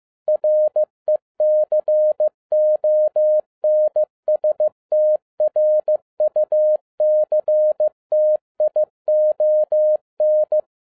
Morse code, heard as a keyed tone, about 15 wpm, 610 Hz. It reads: RECONSTRUCTION
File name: 2012-01-mp3 / n13-A NEWLY DESIGNED QSL CARD-15wpm